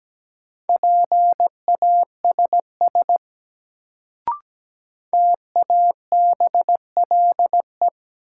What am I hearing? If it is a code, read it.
Morse code, heard as a keyed tone, about 17 wpm, 700 Hz. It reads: PASS TABLE
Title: Sentences from Top 400 Words - New Word Drill - 17wpm